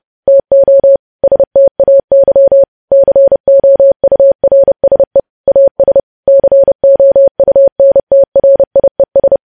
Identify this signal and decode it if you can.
Morse code, heard as a keyed tone, about 30 wpm, 550 Hz. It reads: TO STAY COURSE AS COUNTRIES
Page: news headlines in morse code at 30 wpm